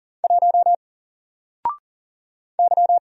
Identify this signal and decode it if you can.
Morse code, heard as a keyed tone, about 40 wpm, 700 Hz. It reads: 1 Y
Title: Single Letter-Number 40wpm